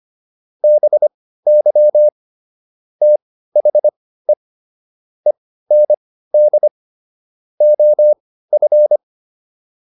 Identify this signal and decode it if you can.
Morse code, heard as a keyed tone, about 25 wpm, spaced as for 15 wpm, 600 Hz. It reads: BY THE END OF